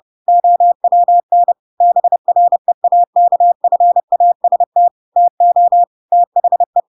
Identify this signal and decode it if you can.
Morse code, heard as a keyed tone, about 30 wpm, 700 Hz. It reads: OWN BREAKFAST TO THE
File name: Pinocchio / Pinocchio ch7 30 30 ik1hge